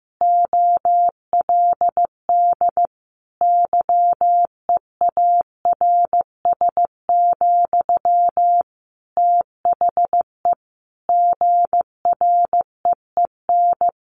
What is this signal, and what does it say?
Morse code, heard as a keyed tone, about 15 wpm, 700 Hz. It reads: OLD YEARS, THE GREEN
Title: A Sound Of Thunder 15wpm